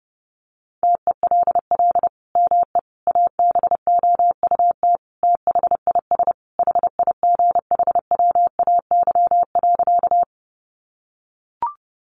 Morse code, heard as a keyed tone, about 30 wpm, 700 Hz. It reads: TELL ME ABOUT THIS HIGHWAY.
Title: Sentences from Top 1400 Words 30wpm